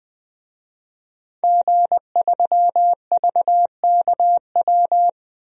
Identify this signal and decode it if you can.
Morse code, heard as a keyed tone, about 20 wpm, 700 Hz. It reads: G3VKW
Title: DXCC-100 Call Signs - 5 Characters - Repeated 2x - 20wpm